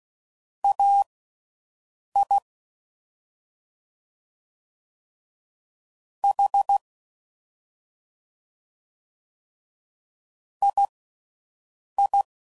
Morse code, heard as a keyed tone, about 16 wpm, spaced as for 6 wpm, 800 Hz. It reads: AI H II
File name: Mixed / CW Comparisons / Comparisons Broken Characters